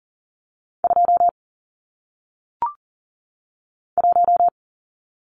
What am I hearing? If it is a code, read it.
Morse code, heard as a keyed tone, about 40 wpm, 700 Hz. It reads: 2 1
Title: Numbers - 1-Digit 40wpm